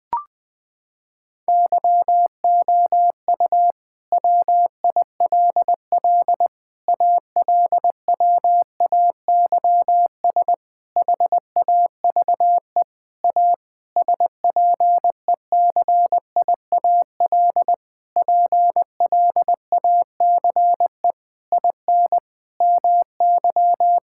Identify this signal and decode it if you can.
Morse code, heard as a keyed tone, about 19 wpm, 700 Hz. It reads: YOU WILL ALWAYS HAVE A SPECIAL PLACE IN MY
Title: Sentences with 9-Word Phrases - 20wpm